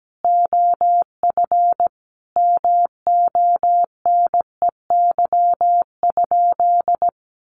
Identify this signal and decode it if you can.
Morse code, heard as a keyed tone, about 17 wpm, 700 Hz. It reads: OF MONEY?
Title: Sentences from Top 1100 Words 17wpm